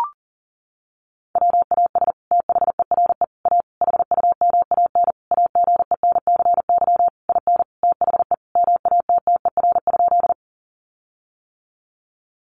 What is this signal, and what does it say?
Morse code, heard as a keyed tone, about 40 wpm, 700 Hz. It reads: WAS THERE A HUMAN AGENCY IN THE MATTER?
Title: Sentences from Top 1300 Words 40wpm